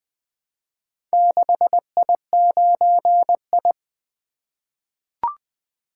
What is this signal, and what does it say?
Morse code, heard as a keyed tone, about 20 wpm, 700 Hz. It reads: 6I9I
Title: Four Letter-Number Combinations - 20wpm